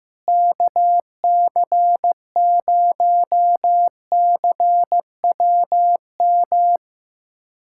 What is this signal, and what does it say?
Morse code, heard as a keyed tone, about 15 wpm, 700 Hz. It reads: KC0CWM
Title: USA Callsign DE Callsign 15wpm